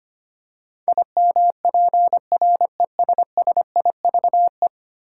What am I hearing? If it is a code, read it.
Morse code, heard as a keyed tone, about 25 wpm, 700 Hz. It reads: IMPRESSIVE
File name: Ten Letter Words - 25wpm